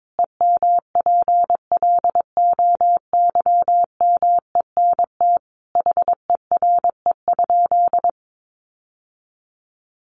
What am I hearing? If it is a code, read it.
Morse code, heard as a keyed tone, about 22 wpm, 700 Hz. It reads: EMPLOYMENT HERE?
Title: Sentences from Top 1500 Words 22wpm